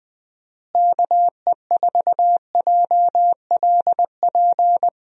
Morse code, heard as a keyed tone, about 21 wpm, 700 Hz. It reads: KE4JLP
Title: USA Call Signs - 6 Characters - Repeated 3x - 20wpm